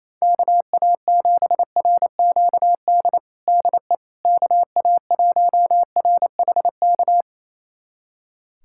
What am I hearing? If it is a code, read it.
Morse code, heard as a keyed tone, about 28 wpm, 700 Hz. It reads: KA7RQD DE KA1RHK
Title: USA Callsign DE Callsign - 6 Characters - 28wpm